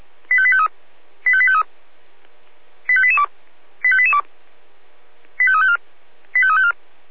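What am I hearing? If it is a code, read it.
Five-tone selective call (ZVEI), heard as quick runs of five tones, tones of about 70 ms. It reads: 85853 85853 85882 85882 85354 85354
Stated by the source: ZVEI five-tone selective call received by SDR.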